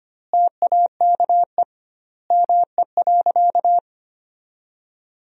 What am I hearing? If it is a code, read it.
Morse code, heard as a keyed tone, about 25 wpm, 700 Hz. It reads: TAKE ME.
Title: 2x Word-Spacing - Sentences from Top 400 Words - 25wpm